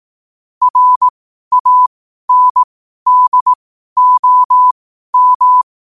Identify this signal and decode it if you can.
Morse code, heard as a keyed tone, about 18 wpm, spaced as for 12 wpm, 1000 Hz. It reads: RANDOM